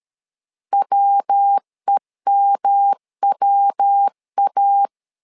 Morse code, heard as a keyed tone, about 13 wpm, 800 Hz. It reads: WEMWA
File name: LEZIONI / 3 MWRO / 3 MW 12wpm 1633